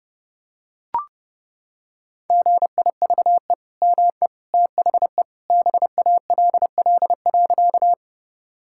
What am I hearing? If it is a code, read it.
Morse code, heard as a keyed tone, about 30 wpm, 700 Hz. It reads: GIVE ME THE BALL.
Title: Sentences from Top 1000 Words - Review Part 6 - 30wpm